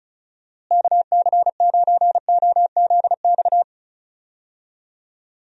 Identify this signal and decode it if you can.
Morse code, heard as a keyed tone, about 36 wpm, 700 Hz. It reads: KC9OZX
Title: USA Call Signs - 6 Characters - Repeated 2x - 35wpm